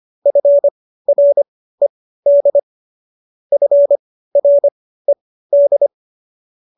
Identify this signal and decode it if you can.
Morse code, heard as a keyed tone, about 25 wpm, spaced as for 15 wpm, 570 Hz. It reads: FRED FRED
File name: qso222 15